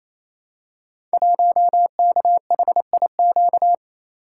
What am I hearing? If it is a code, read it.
Morse code, heard as a keyed tone, about 29 wpm, 700 Hz. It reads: 1KHIQ
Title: Code Groups - 5 Letter:Numbers - 28wpm